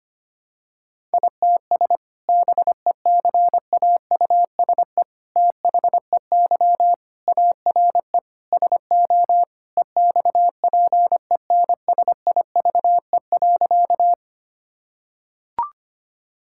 Morse code, heard as a keyed tone, about 25 wpm, 700 Hz. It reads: ITS BECAUSE THEY ARE SO EXPENSIVE.
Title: Sentences from Top 1000 Words - Review Part 6 - 25wpm